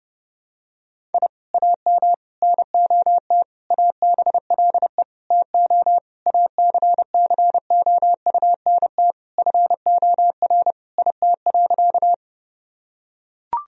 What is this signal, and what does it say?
Morse code, heard as a keyed tone, about 32 wpm, 700 Hz. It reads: I AM NOT ABLE TO ACCOUNT FOR IT.
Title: Sentences from Top 1000 Words - Review Part 2 - 30wpm